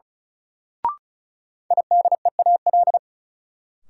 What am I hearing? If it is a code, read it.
Morse code, heard as a keyed tone, about 36 wpm, 700 Hz. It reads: IDEAL
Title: Sentences from Top 1400 Words - New Word Drill - 35wpm